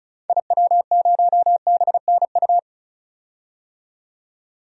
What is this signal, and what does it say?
Morse code, heard as a keyed tone, about 34 wpm, 700 Hz. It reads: IW0BNU
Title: DXCC-100 Call Signs - 6 Characters - 35wpm